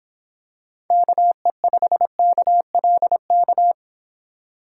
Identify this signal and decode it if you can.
Morse code, heard as a keyed tone, about 26 wpm, 700 Hz. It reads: KE5KLK